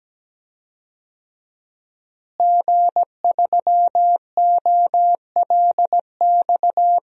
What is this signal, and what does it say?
Morse code, heard as a keyed tone, about 17 wpm, 700 Hz. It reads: G3OLX
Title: DXCC-100 Call Signs - 5 Characters - Repeated 3x - 17wpm